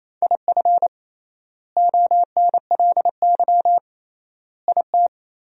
Morse code, heard as a keyed tone, about 28 wpm, 700 Hz. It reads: IF ONLY IT
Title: 3x Word-Spacing - Sentences from Top 500 Words - 28wpm